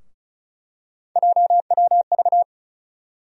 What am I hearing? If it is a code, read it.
Morse code, heard as a keyed tone, about 34 wpm, 700 Hz. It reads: JWV
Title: USA Call Sign Postfixes - 3 Letters 35wpm